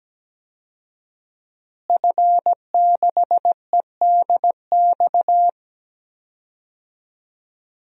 Morse code, heard as a keyed tone, about 17 wpm, 700 Hz. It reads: F6EDX
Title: DXCC-100 Call Signs - 5 Characters - Repeated 3x - 17wpm